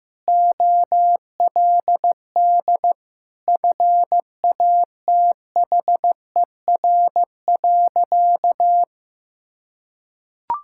Morse code, heard as a keyed tone, about 16 wpm, 700 Hz. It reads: OLD FATHER.
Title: Sentences from Top 1000 Words - Review Part 5 - 15wpm